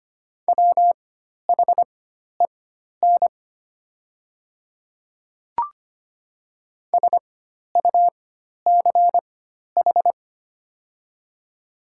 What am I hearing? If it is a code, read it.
Morse code, heard as a keyed tone, about 25 wpm, spaced as for 12 wpm, 700 Hz. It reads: WHEN SUCH
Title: 4x Character Spacing - Top 300 Words - 25wpm